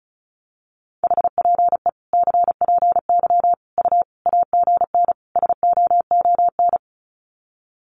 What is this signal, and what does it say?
Morse code, heard as a keyed tone, about 34 wpm, 700 Hz. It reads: HPE CPY U AGN SOON
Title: Common QSO Phrases 35wpm